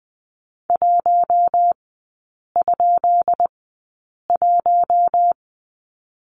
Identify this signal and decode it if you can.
Morse code, heard as a keyed tone, about 20 wpm, spaced as for 8 wpm, 700 Hz. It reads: 1?1
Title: 184 Lesson 30e - Letter b - Three Characters 20wpm